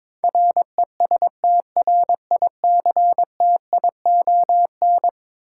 Morse code, heard as a keyed tone, about 22 wpm, 700 Hz. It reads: RESTRICTION
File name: practice / 22wpm / Eleven Letter Words - 22wpm